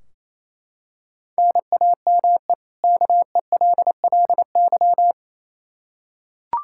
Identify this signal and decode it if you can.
Morse code, heard as a keyed tone, about 27 wpm, 700 Hz. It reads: NAME KELLY
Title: K1USN SST QSO Practice 28wpm